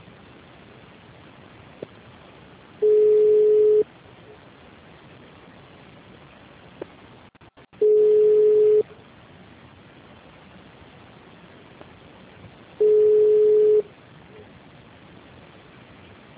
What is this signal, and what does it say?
Ringback tone (Europe, 425 Hz).
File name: out- -2034- - - .72